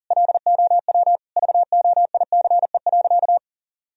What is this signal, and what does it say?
Morse code, heard as a keyed tone, about 40 wpm, 700 Hz. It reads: LOW VOICE.
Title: Sentences from Top 1000 Words - Review Part 1 - 40wpm